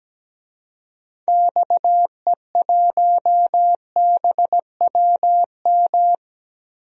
Morse code, heard as a keyed tone, about 17 wpm, 700 Hz. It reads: XE1BWM
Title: DXCC-100 Call Signs - 6 Characters - 17wpm